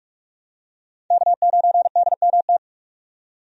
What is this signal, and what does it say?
Morse code, heard as a keyed tone, about 48 wpm, 700 Hz. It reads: K9DMT
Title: USA Call Signs - 5 Characters - 45wpm